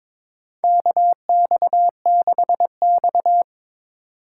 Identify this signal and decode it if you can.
Morse code, heard as a keyed tone, about 21 wpm, 700 Hz. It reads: KX6X
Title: K1USN SST QSO Practice - 22wpm